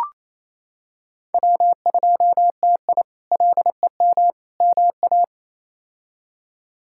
Morse code, heard as a keyed tone, about 27 wpm, 700 Hz. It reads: W2TI LEM MA
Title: K1USN SST QSO Practice - 28wpm